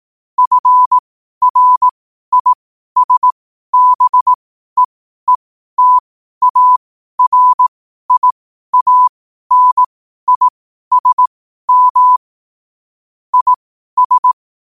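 Morse code, heard as a keyed tone, about 18 wpm, spaced as for 12 wpm, 1000 Hz. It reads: FRISBEETARIANISM IS
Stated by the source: Morse code at 12 wpm.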